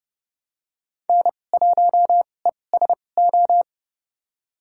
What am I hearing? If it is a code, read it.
Morse code, heard as a keyed tone, about 30 wpm, spaced as for 22 wpm, 700 Hz. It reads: N1ESO
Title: USA Call Signs - 5 Characters - 2x Character Spacing - 30wpm